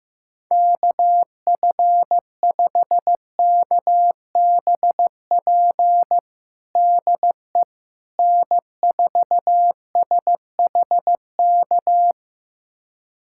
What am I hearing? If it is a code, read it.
Morse code, heard as a keyed tone, about 15 wpm, 700 Hz. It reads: KF5KBP DE N4SHK
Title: USA Callsign DE Callsign 15wpm